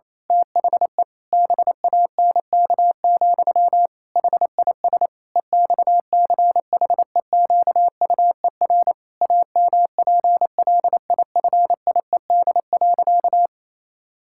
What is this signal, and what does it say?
Morse code, heard as a keyed tone, about 28 wpm, 700 Hz. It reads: THE BANK, HIS EXCHEQUER AMPLIFIED.
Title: Pangram Challenge v2 - 28wpm